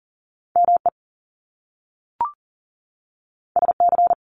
Morse code, heard as a keyed tone, about 42 wpm, 700 Hz. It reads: ME SC
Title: Top Two Letter Combinations in Words 40wpm